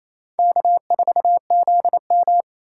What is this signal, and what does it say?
Morse code, heard as a keyed tone, about 28 wpm, 700 Hz. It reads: K4ZM
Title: USA Call Signs - 6x Character Spacing - 28wpm